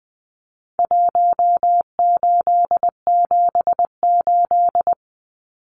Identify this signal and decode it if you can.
Morse code, heard as a keyed tone, about 20 wpm, 700 Hz. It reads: 1878
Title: Numbers - Contest Style - 4-Digit 20wpm